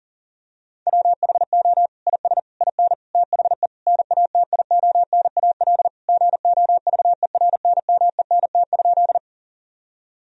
Morse code, heard as a keyed tone, about 40 wpm, 700 Hz. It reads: WHO IS IN THE NATIONAL GOVERNMENT?
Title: Sentences from Top 1000 Words - Review Part 4 - 40wpm